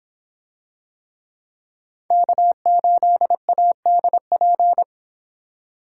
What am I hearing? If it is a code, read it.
Morse code, heard as a keyed tone, about 24 wpm, 700 Hz. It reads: K8ADP